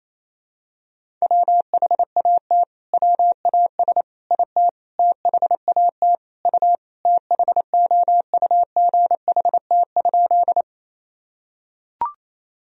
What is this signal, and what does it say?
Morse code, heard as a keyed tone, about 28 wpm, 700 Hz. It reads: WHAT WAS IT THAT U THOUGHT?
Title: Sentences from Top 300 Words and Abbreviations 28wpm